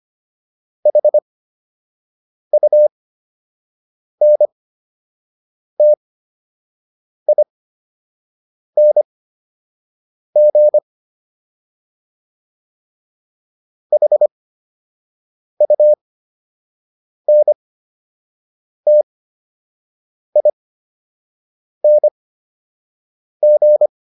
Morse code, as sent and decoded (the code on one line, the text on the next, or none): .... ..- -. - .. -. --. / .... ..- -. - .. -. --.
HUNTING HUNTING